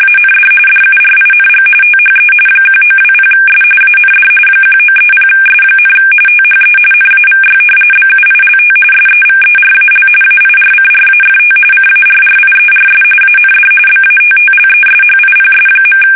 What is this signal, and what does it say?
RTTY, about 45.45 baud. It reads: AND DURING THE 1920S AND 1930S, THE NAVY BUILT SEVERAL AIRCRAFT CARRIERS AND BATTLESHIPS.
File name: fsk850 / usnavy1